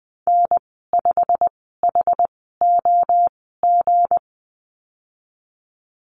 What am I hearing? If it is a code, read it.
Morse code, heard as a keyed tone, about 20 wpm, spaced as for 15 wpm, 700 Hz. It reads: N5HOG
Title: 161 Lesson 26g - Letter g - Call Signs 20wpm